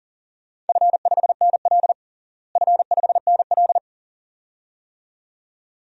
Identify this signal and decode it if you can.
Morse code, heard as a keyed tone, about 40 wpm, 700 Hz. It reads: F5NL F5NL
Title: DXCC-100 Call Signs - 4 Characters - Repeated 2x - 40wpm